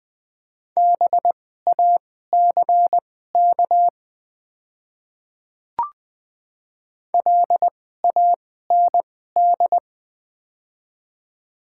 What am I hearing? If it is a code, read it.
Morse code, heard as a keyed tone, about 20 wpm, spaced as for 14 wpm, 700 Hz. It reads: BACK LAND
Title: 2x Character Spacing - Top 200 Words - 20wpm